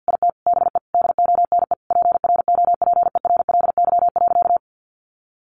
Morse code, heard as a keyed tone, about 50 wpm, 700 Hz. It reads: IT BE DONE PROPERLY.
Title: Sentences from Top 1100 Words 50wpm